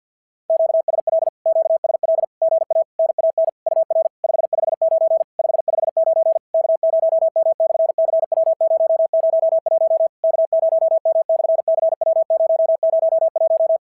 Morse code, heard as a keyed tone, about 51 wpm, 690 Hz. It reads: QSL QSL GA NAN UR 559 559 K0M/CW001 K0M/CW001
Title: SOTA QSO Practice 50wpm